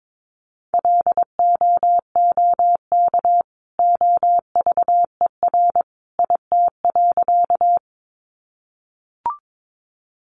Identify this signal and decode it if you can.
Morse code, heard as a keyed tone, about 22 wpm, 700 Hz. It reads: LOOK OVER IT.
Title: Sentences from Top 100 Words 22wpm